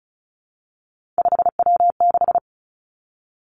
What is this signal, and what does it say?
Morse code, heard as a keyed tone, about 36 wpm, 700 Hz. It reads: 5W6
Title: Three Challenging Letter-Number Combinations 35wpm